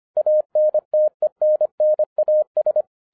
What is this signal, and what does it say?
Morse code, heard as a keyed tone, about 25 wpm, 610 Hz. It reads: ANTENNAS
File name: 2012-01-mp3 / n14-FUN WITH NUMBERS-25wpm